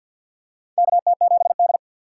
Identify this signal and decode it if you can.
Morse code, heard as a keyed tone, about 48 wpm, 700 Hz. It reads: KT7D
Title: USA Call Signs - 4 Characters - 3x Character Spacing - 50wpm